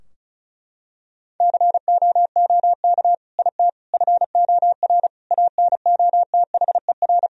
Morse code, heard as a keyed tone, about 34 wpm, 700 Hz. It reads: COOK IT FOR ANOTHER
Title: Sentences from Top 1000 Words - Review Part 4 - 35wpm